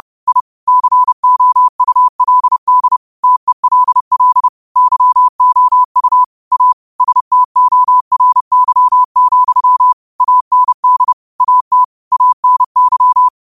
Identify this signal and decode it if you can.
Morse code, heard as a keyed tone, about 30 wpm, 1000 Hz. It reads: I COULD TELL YOU A STORY, AND AT ANY